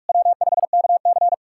Morse code, heard as a keyed tone, about 46 wpm, 700 Hz. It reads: W5KC
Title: USA Callsign DE Callsign - 4 Characters - 45wpm